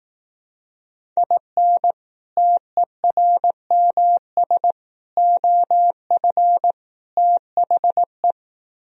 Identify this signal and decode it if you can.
Morse code, heard as a keyed tone, about 18 wpm, 700 Hz. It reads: IN TERMS OF THE
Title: Speed-Racing - Sets of 4 Words 15wpm